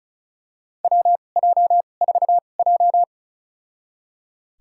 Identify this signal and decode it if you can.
Morse code, heard as a keyed tone, about 34 wpm, spaced as for 25 wpm, 700 Hz. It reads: WJ4J
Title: USA Call Signs - 2x Character Spacing - 35wpm